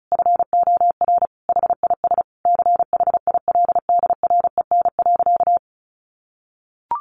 Morse code, heard as a keyed tone, about 34 wpm, 700 Hz. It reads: FOR HIS CHILDREN.
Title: Sentences from Top 1100 Words 35wpm